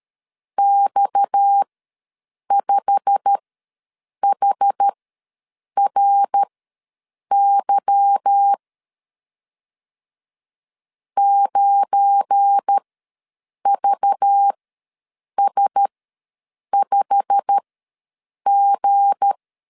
Morse code, heard as a keyed tone, about 13 wpm, spaced as for 7 wpm, 800 Hz. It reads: X5HRY 9VS5G
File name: LEZIONI / 8 YJ45 / 8 YJ45 ws 12wpm 3146